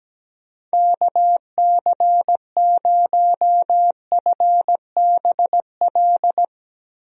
Morse code, heard as a keyed tone, about 17 wpm, 700 Hz. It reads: KC0FBL
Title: USA Call Signs - 6 Characters - Repeated 2x - 17wpm